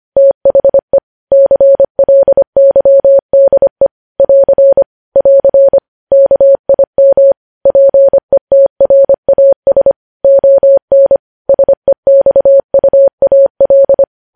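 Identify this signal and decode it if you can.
Morse code, heard as a keyed tone, about 25 wpm, 550 Hz. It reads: THE CLYDE <AR> <AR> KIM PETRAS ON SEXUAL
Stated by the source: Morse code practice of news headlines at 25 wpm, opening with vvv.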